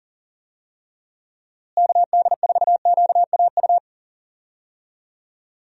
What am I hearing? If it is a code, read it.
Morse code, heard as a keyed tone, about 40 wpm, 700 Hz. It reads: KD4QAU
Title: USA Call Signs - 6 Characters - Repeated 3x - 40wpm